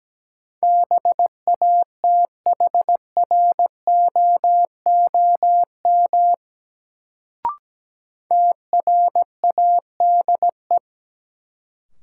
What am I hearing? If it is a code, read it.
Morse code, heard as a keyed tone, about 17 wpm, 700 Hz. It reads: BATHROOM TRADE
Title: Sets of 5 Words Encore - New Word Drill - 17wpm